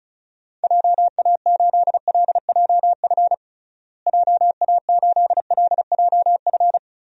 Morse code, heard as a keyed tone, about 36 wpm, 700 Hz. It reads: JA8LJF JA8LJF
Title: DXCC-100 Call Signs - 6 Characters - Repeated 3x - 35wpm